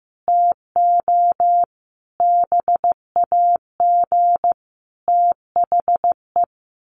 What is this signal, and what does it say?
Morse code, heard as a keyed tone, about 15 wpm, 700 Hz. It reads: TO BAG THE
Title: A Sound Of Thunder 15wpm